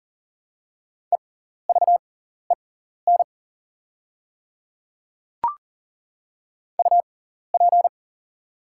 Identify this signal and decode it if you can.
Morse code, heard as a keyed tone, about 39 wpm, spaced as for 14 wpm, 690 Hz. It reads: EVEN UP
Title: 6x Character Spacing - Top 300 Words - 40wpm